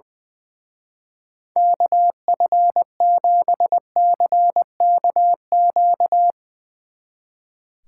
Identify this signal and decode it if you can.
Morse code, heard as a keyed tone, about 20 wpm, 700 Hz. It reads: KF7CKQ
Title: Speed Racing - USA 2x3 Call Signs 20wpm